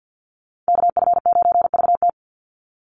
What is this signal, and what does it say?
Morse code, heard as a keyed tone, about 48 wpm, 700 Hz. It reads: KF9VT
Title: USA Call Signs 50wpm